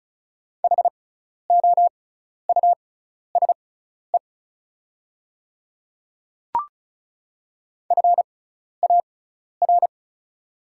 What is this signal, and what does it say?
Morse code, heard as a keyed tone, about 36 wpm, spaced as for 12 wpm, 700 Hz. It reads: HOUSE FAR
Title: 6x Character Spacing - Top 300 Words - 35wpm